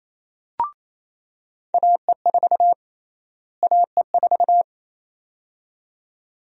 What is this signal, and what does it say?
Morse code, heard as a keyed tone, about 28 wpm, 700 Hz. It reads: AE4 AE4
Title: USA Call Sign Prefixes - Repeated 2x - 28wpm